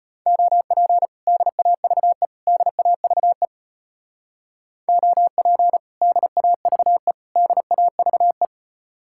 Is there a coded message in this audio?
Yes: Morse code from 0.3 to 8.4 s, heard as a keyed tone, about 33 wpm, 700 Hz. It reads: OP DAVE DAVE OP DAVE DAVE